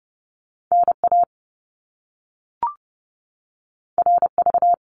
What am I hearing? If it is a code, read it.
Morse code, heard as a keyed tone, about 30 wpm, 700 Hz. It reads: NA RV
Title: Top Two Letter Combinations in Words 30wpm